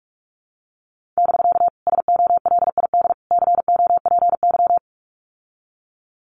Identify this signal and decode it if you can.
Morse code, heard as a keyed tone, about 44 wpm, 700 Hz. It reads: <BK> SOLID COPY
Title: Common QSO Phrases Encore 45wpm